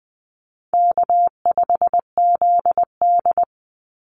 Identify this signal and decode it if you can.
Morse code, heard as a keyed tone, about 20 wpm, 700 Hz. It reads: K5ZD
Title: Easy Call Signs Repeated 2x - 20wpm